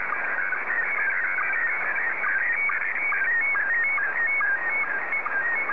SSTV (slow-scan television), Scottie 1.